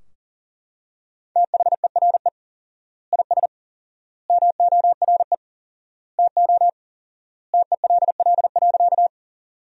Morse code, heard as a keyed tone, about 40 wpm, 700 Hz. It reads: THERE IS MORE TO TELL.
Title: 4x Word-Spacing - Sentences from Top 200 Words - 40wpm